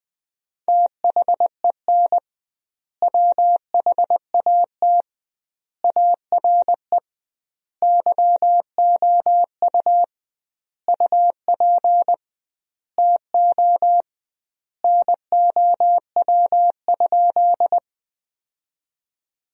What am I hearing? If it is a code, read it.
Morse code, heard as a keyed tone, about 20 wpm, 700 Hz. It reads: THEN WHAT ARE YOU UP TO NOW?
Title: 2x Word-Spacing - Sentences from Top 100 Words - 20wpm